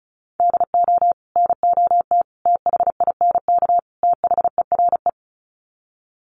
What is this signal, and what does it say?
Morse code, heard as a keyed tone, about 34 wpm, 700 Hz. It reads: DO NOT THINK THERE
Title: Sets of 4 Words 35wpm